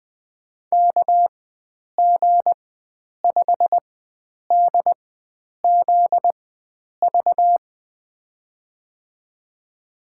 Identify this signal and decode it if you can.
Morse code, heard as a keyed tone, about 21 wpm, spaced as for 9 wpm, 700 Hz. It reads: KG5DZV
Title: USA Call Signs - 6 Characters - 4x Character Spacing - 20wpm